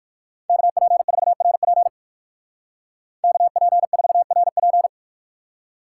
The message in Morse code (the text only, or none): KP4RP KP4RP